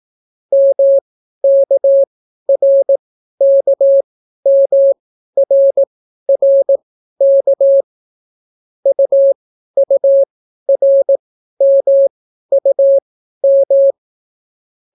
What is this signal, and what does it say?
Morse code, heard as a keyed tone, about 18 wpm, spaced as for 12 wpm, 550 Hz. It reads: MKRKMRRK UURMUM